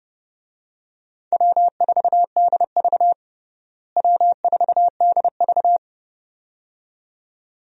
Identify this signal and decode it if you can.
Morse code, heard as a keyed tone, about 32 wpm, 700 Hz. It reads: W4DV W4DV
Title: USA Call Signs - 4 Characters - Repeated 2x - 30wpm